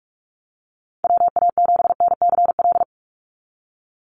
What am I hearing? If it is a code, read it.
Morse code, heard as a keyed tone, about 44 wpm, 700 Hz. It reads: WA7NCL
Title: NA SOTA Activator Callsigns 45wpm